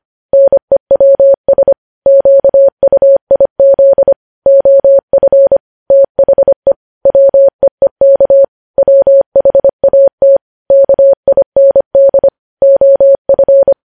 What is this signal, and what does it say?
Morse code, heard as a keyed tone, about 25 wpm, 550 Hz. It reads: NEWS QUIZ OF THE WEEK WHAT KIND OF